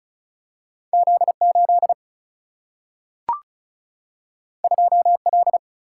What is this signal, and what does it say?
Morse code, heard as a keyed tone, about 36 wpm, 700 Hz. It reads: Z 8 2 L
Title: Single-Letter Single-Number 35wpm